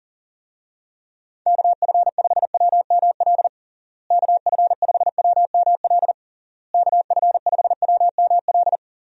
Morse code, heard as a keyed tone, about 40 wpm, 700 Hz. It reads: KF5WML KF5WML KF5WML
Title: USA Call Signs - 6 Characters - Repeated 3x - 40wpm